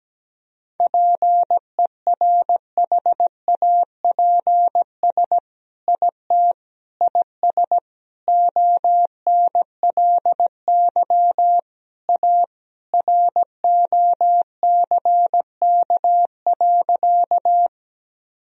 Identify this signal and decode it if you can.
Morse code, heard as a keyed tone, about 17 wpm, 700 Hz. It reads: PERHAPS IT IS ONLY A ROCK.
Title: Sentences from Top 1000 Words - Review Part 2 - 17wpm